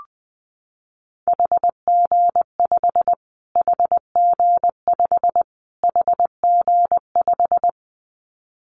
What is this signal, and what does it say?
Morse code, heard as a keyed tone, about 20 wpm, 700 Hz. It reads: HG5 HG5 HG5
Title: Callsign Prefixes for Top 25 DXCC - Repeated 3x - 20wpm